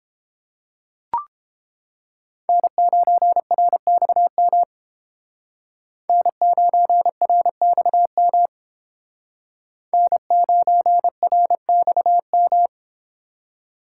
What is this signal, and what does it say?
Morse code, heard as a keyed tone, about 30 wpm, 700 Hz. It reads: N9RXM N9RXM N9RXM